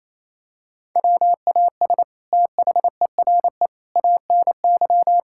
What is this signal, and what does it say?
Morse code, heard as a keyed tone, about 28 wpm, 700 Hz. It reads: WAS THERE ANY
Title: Sentences from Top 1600 Words 28wpm